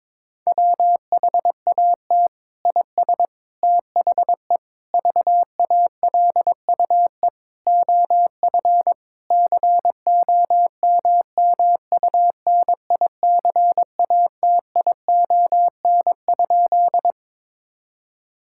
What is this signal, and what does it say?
Morse code, heard as a keyed tone, about 22 wpm, 700 Hz. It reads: WHAT IS THE VALUE OF COMMUNICATION?
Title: Sentences from Top 1000 Words - Review Part 6 - 22wpm